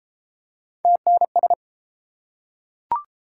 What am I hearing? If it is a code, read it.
Morse code, heard as a keyed tone, about 31 wpm, 700 Hz. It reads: TNS